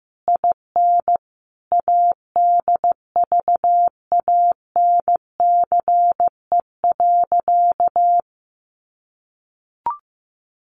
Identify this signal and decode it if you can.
Morse code, heard as a keyed tone, about 15 wpm, 700 Hz. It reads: IN ADVANCE.
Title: Sentences from Top 1400 Words 15wpm